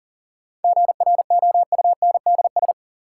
Morse code, heard as a keyed tone, about 40 wpm, 700 Hz. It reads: GROUNDS
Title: Sentences from Top 1300 Words - New Word Drill Alt - 40wpm